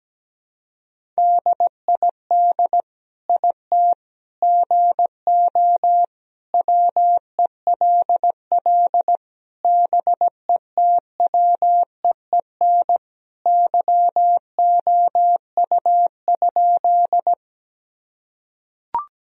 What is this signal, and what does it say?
Morse code, heard as a keyed tone, about 17 wpm, 700 Hz. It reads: DID IT GO WELL BETWEEN YOU?
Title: Sentences from Top 1000 Words - Review Part 1 - 17wpm